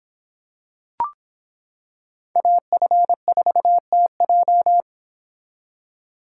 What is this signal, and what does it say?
Morse code, heard as a keyed tone, about 26 wpm, 700 Hz. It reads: AF4TJ